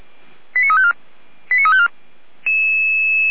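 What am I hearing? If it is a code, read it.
Five-tone selective call (ZVEI), heard as quick runs of five tones, tones of about 70 ms. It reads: 89354 89354
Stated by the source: ZVEI five-tone selective call received on an SDR, from a fire brigade's alerting.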